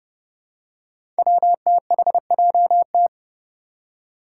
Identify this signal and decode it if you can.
Morse code, heard as a keyed tone, about 29 wpm, 700 Hz. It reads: WTHJT
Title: Five Challenging Letter-Number Combinations 30wpm